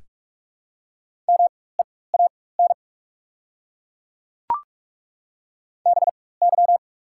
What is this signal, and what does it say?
Morse code, heard as a keyed tone, about 42 wpm, spaced as for 20 wpm, 690 Hz. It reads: MEAN BY
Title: 4x Character Spacing - Top 200 Words - 45wpm